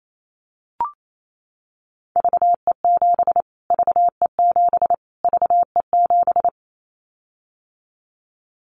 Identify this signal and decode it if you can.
Morse code, heard as a keyed tone, about 28 wpm, 700 Hz. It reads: VE7 VE7 VE7
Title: Callsign Prefixes for Top 25 DXCC - Repeated 3x - 28wpm